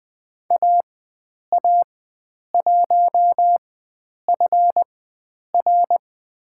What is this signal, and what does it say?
Morse code, heard as a keyed tone, about 20 wpm, spaced as for 9 wpm, 700 Hz. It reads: AA1FR
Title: USA Call Signs - 5 Characters - 4x Character Spacing - 20wpm